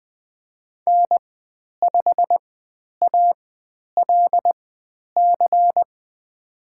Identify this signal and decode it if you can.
Morse code, heard as a keyed tone, about 20 wpm, spaced as for 10 wpm, 700 Hz. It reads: N5ALC
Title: 095 Lesson 16g - Letter c - Call Signs 20wpm